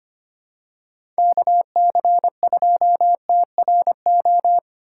Morse code, heard as a keyed tone, about 25 wpm, 700 Hz. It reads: KC2TRO
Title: USA Call Signs - 6 Characters - 6x Character Spacing - 25wpm